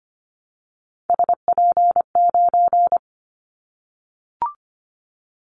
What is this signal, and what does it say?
Morse code, heard as a keyed tone, about 25 wpm, 700 Hz. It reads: SP9
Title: Callsign Prefixes for Top 25 DXCC - 25wpm